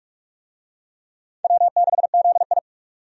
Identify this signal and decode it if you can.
Morse code, heard as a keyed tone, about 46 wpm, 700 Hz. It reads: W6ZI
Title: USA Call Signs - 4 Characters - 45wpm